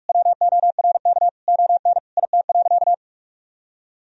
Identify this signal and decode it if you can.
Morse code, heard as a keyed tone, about 41 wpm, 700 Hz. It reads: WORK ON IT.
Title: Sentences from Top 1000 Words - Review Part 3 - 45wpm